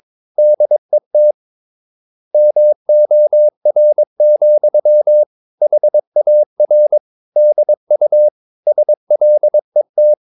Morse code, heard as a keyed tone, about 22 wpm, 600 Hz. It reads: DET MOR, HAR DU SLET
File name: Ferie QRS 22wpm